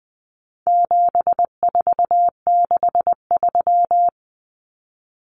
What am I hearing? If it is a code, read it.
Morse code, heard as a keyed tone, about 20 wpm, 700 Hz. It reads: 7463
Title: Common Cut Numbers - 4-Digit 20wpm